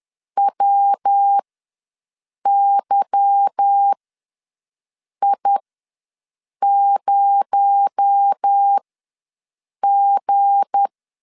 Morse code, heard as a keyed tone, about 10 wpm, spaced as for 6 wpm, 800 Hz. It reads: WYI0G
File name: LEZIONI / 10 SIGNS / 10D COMA ws 10wpm 4314